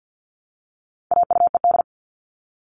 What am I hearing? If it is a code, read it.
Morse code, heard as a keyed tone, about 50 wpm, 700 Hz. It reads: AUED
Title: All Cut Numbers - 4-Digit 50wpm